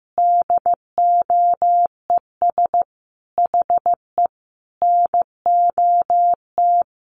Morse code, heard as a keyed tone, about 15 wpm, 700 Hz. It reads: DOES HE NOT
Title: Sentences from Top 1400 Words 15wpm